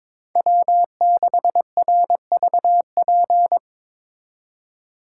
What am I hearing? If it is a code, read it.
Morse code, heard as a keyed tone, about 25 wpm, 700 Hz. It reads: W6RVP
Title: Speed Racing - USA Call Signs 22wpm